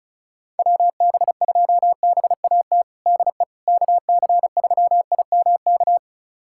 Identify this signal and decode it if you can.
Morse code, heard as a keyed tone, about 36 wpm, 700 Hz. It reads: WB2BAT DE KC3IMK
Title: USA Callsign DE Callsign - 6 Characters - 35wpm